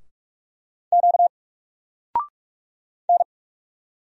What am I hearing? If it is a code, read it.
Morse code, heard as a keyed tone, about 44 wpm, 700 Hz. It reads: Q N
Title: Single Letter-Number Alt 45wpm